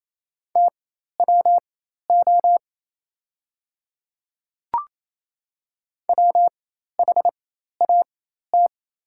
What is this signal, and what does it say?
Morse code, heard as a keyed tone, about 28 wpm, spaced as for 13 wpm, 700 Hz. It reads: TWO WHAT
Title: 4x Character Spacing - Top 100 Words - 28wpm